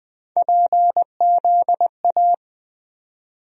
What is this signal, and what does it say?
Morse code, heard as a keyed tone, about 20 wpm, 700 Hz. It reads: PZA
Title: USA Call Sign Postfixes - 3 Letters 20wpm